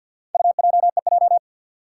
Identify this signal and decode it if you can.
Morse code, heard as a keyed tone, about 52 wpm, 700 Hz. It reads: UJEJ
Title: Four Challenging Letter-Number Combinations 50wpm